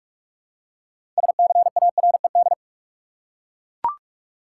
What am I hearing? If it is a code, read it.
Morse code, heard as a keyed tone, about 41 wpm, 700 Hz. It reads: SCARED
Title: Six Letter Words - 45wpm